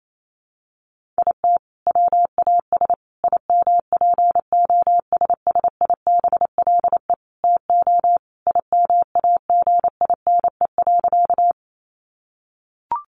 Morse code, heard as a keyed tone, about 28 wpm, 700 Hz. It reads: IT WAS IMPOSSIBLE TO IMAGINE.
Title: Sentences from Top 1400 Words 28wpm